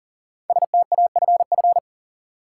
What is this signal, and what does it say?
Morse code, heard as a keyed tone, about 40 wpm, 700 Hz. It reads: STAFF
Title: Five Letter Words - 40wpm